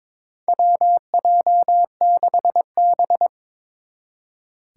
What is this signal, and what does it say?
Morse code, heard as a keyed tone, about 22 wpm, 700 Hz. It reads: WJ6B
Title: Four Challenging Letter-Number Combinations 22wpm